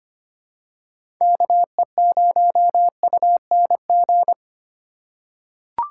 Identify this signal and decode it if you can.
Morse code, heard as a keyed tone, about 25 wpm, 700 Hz. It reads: KE0UNG
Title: USA Call Signs - 6 Characters - 3x Character Spacing - 25wpm